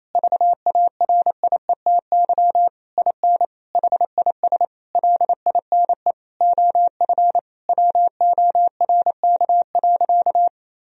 Morse code, heard as a keyed tone, about 28 wpm, 700 Hz. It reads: VARIETY IN HIS LINE OF WORK.
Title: Sentences from Top 1000 Words - Review Part 4 - 28wpm